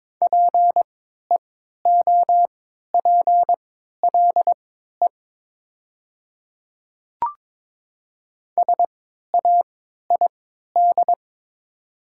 Morse code, heard as a keyed tone, about 22 wpm, spaced as for 12 wpm, 700 Hz. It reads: PEOPLE SAID
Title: 3x Character Spacing - Top 100 Words - 22wpm